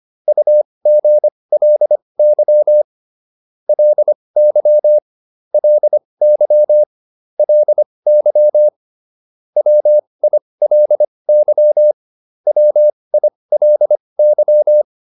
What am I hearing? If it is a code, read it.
Morse code, heard as a keyed tone, about 25 wpm, spaced as for 20 wpm, 600 Hz. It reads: UGLY LY LY LY WILY WILY